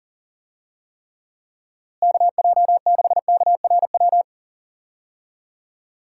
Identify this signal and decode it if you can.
Morse code, heard as a keyed tone, about 40 wpm, 700 Hz. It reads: KJ6KRW
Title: USA Call Signs - 6 Characters - Repeated 3x - 40wpm